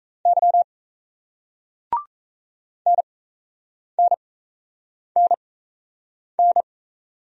Morse code, heard as a keyed tone, about 33 wpm, spaced as for 8 wpm, 700 Hz. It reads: Y NNNN
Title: Speed-Racing Single Letters 28wpm